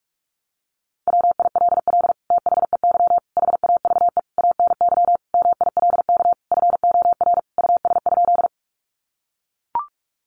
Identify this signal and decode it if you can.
Morse code, heard as a keyed tone, about 44 wpm, 700 Hz. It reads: WILL THEY HAVE ANY MILK FOR US?
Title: Sentences from Top 1500 Words 45wpm